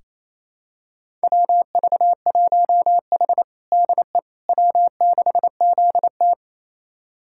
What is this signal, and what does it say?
Morse code, heard as a keyed tone, about 27 wpm, 700 Hz. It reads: WV1H DE W6ZT
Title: USA Callsign DE Callsign - 4 Characters - 28wpm